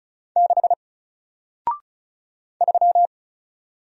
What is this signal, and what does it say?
Morse code, heard as a keyed tone, about 35 wpm, 700 Hz. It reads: 6 3
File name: Single-Letter-Number-35wpm